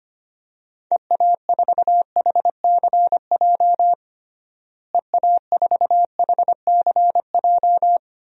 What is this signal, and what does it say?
Morse code, heard as a keyed tone, about 25 wpm, 700 Hz. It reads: EA4HCJ EA4HCJ
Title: DXCC-100 Call Signs - 6 Characters - Repeated 2x - 25wpm